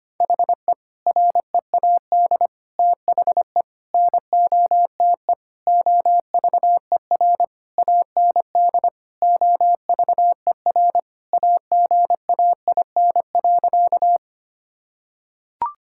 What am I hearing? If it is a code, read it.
Morse code, heard as a keyed tone, about 25 wpm, 700 Hz. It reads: HE READ THE NOTE OVER AND OVER AGAIN.
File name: practice / 25wpm / Sentences with 4-Word Phrases - 25wpm